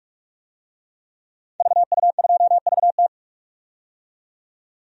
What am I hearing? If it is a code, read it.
Morse code, heard as a keyed tone, about 41 wpm, 700 Hz. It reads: VU2VT
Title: DXCC-100 Call Signs - 5 Characters - 45wpm